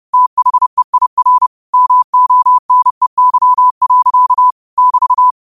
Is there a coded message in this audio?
Yes: Morse code from 0.1 to 5.3 s, heard as a keyed tone, about 30 wpm, 1000 Hz. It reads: THEIR MONEY. <BT>